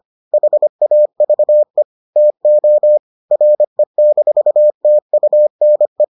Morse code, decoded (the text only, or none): HAVE TO RE-TUNE